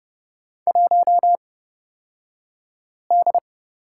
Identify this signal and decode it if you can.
Morse code, heard as a keyed tone, about 30 wpm, spaced as for 5 wpm, 700 Hz. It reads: 1D
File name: Lesson 38d - Letter q - Two Characters 30wpm